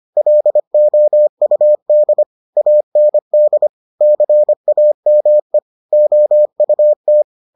Morse code, heard as a keyed tone, about 25 wpm, 600 Hz. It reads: LOUD AND CAME OUT